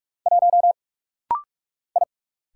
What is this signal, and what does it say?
Morse code, heard as a keyed tone, about 44 wpm, 700 Hz. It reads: 1 I
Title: Single Letter-Number Alt - Rapid-Fire 45wpm